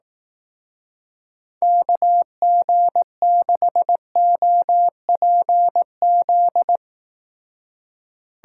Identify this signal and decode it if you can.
Morse code, heard as a keyed tone, about 18 wpm, 700 Hz. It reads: KG6OPZ
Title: Speed Racing - USA 2x3 Call Signs 15wpm